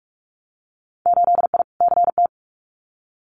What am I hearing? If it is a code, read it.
Morse code, heard as a keyed tone, about 46 wpm, 700 Hz. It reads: 8I CT
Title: Winter Field Day Exchanges Encore 45wpm